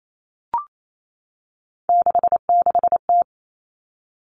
Morse code, heard as a keyed tone, about 28 wpm, 700 Hz. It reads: 66T
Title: Common Cut Numbers - 3-Digit 28wpm